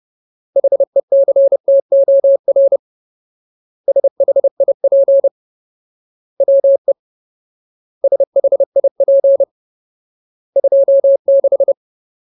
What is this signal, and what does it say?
Morse code, heard as a keyed tone, about 30 wpm, 550 Hz. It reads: HECTOR SHIP WE SHIP 26
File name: Ultimate Word List / 30WPM Ultimate